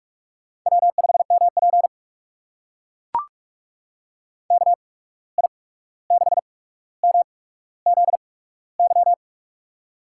Morse code, heard as a keyed tone, about 44 wpm, spaced as for 17 wpm, 700 Hz. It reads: W5MP KI6MZY
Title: USA Call Signs - 8x Character Spacing - 45wpm